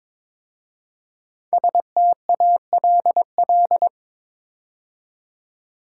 Morse code, heard as a keyed tone, about 23 wpm, 700 Hz. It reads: STALL
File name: 5 Letter Words - First 3 Letters Given - 22wpm